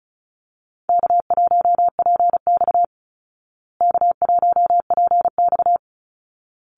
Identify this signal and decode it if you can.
Morse code, heard as a keyed tone, about 34 wpm, 700 Hz. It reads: K1PX K1PX
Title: Easy Call Signs Repeated 2x - 35wpm